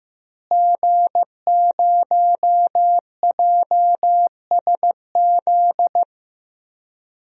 Morse code, heard as a keyed tone, about 15 wpm, 700 Hz. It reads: G0JSZ
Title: DXCC-100 Call Signs - 5 Characters - Repeated 2x - 15wpm